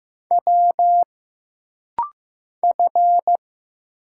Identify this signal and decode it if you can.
Morse code, heard as a keyed tone, about 15 wpm, 700 Hz. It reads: W F
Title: Single Letter-Number Alt - Rapid-Fire 15wpm